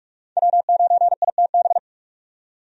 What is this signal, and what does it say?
Morse code, heard as a keyed tone, about 46 wpm, 700 Hz. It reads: W9ITB
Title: USA Call Signs - 5 Characters - 45wpm